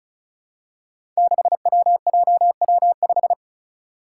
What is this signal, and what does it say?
Morse code, heard as a keyed tone, about 36 wpm, 700 Hz. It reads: 6WJW5
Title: Five Challenging Letter-Number Combinations 35wpm